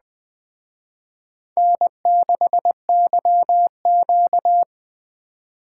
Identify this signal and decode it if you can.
Morse code, heard as a keyed tone, about 19 wpm, 700 Hz. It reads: N6YQ